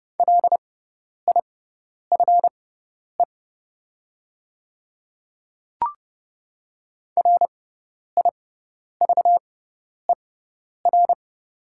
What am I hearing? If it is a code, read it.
Morse code, heard as a keyed tone, about 30 wpm, spaced as for 10 wpm, 690 Hz. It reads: LIFE RIVER
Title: 6x Character Spacing - Top 500 Words - 30wpm